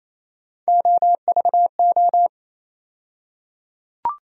Morse code, heard as a keyed tone, about 28 wpm, 700 Hz. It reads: OVO
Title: USA Call Sign Postfixes - 3 Letters 28wpm